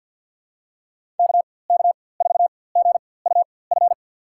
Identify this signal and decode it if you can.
Morse code, heard as a keyed tone, about 51 wpm, spaced as for 23 wpm, 700 Hz. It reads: KK4GUF
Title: USA Call Signs - 6 Characters - 4x Character Spacing - 50wpm